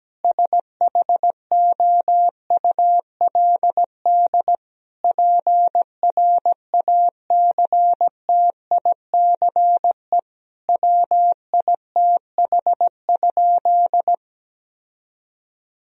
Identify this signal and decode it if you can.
Morse code, heard as a keyed tone, about 17 wpm, 700 Hz. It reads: SHOULD PRACTICE WITH?
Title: Sentences from Top 1000 Words - Review Part 4 - 17wpm